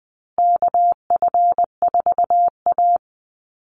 Morse code, heard as a keyed tone, about 20 wpm, 700 Hz. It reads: KF4A
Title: Call Signs Repeated 2x - 20wpm